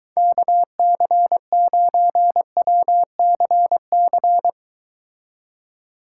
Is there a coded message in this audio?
Yes: Morse code from 0.2 to 4.5 s, heard as a keyed tone, about 22 wpm, 700 Hz. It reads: KC9WCC